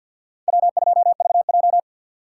Morse code, heard as a keyed tone, about 47 wpm, 700 Hz. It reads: W2VJ
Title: USA Call Signs - 4 Characters - 3x Character Spacing - 50wpm